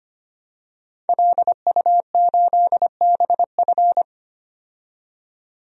Morse code, heard as a keyed tone, about 25 wpm, 700 Hz. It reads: LU8BF
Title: DXCC-100 Call Signs - 5 Characters - 25wpm